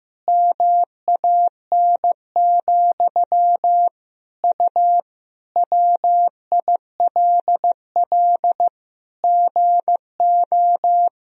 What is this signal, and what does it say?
Morse code, heard as a keyed tone, about 14 wpm, 700 Hz. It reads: MAN, U WILL GO
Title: Sentences from Top 400 Words and Abbreviations 15wpm